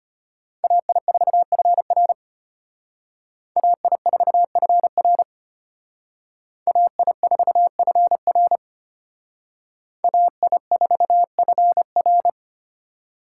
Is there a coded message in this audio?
Yes: Morse code from 0.6 to 12.3 s, heard as a keyed tone, about 33 wpm, 700 Hz. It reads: AI4FR AI4FR AI4FR AI4FR